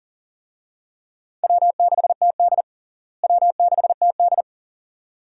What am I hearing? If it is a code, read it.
Morse code, heard as a keyed tone, about 38 wpm, 700 Hz. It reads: W6TD W6TD
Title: USA Call Signs - 4 Characters - Repeated 2x - 40wpm